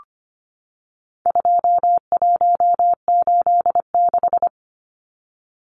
Morse code, heard as a keyed tone, about 25 wpm, 700 Hz. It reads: 2186
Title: Numbers - 4-Digit 25wpm